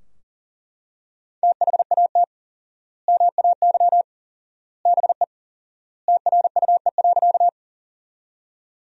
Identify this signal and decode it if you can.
Morse code, heard as a keyed tone, about 37 wpm, 700 Hz. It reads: THAT MAY BE TRUE.
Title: 4x Word-Spacing - Sentences from Top 400 Words - 40wpm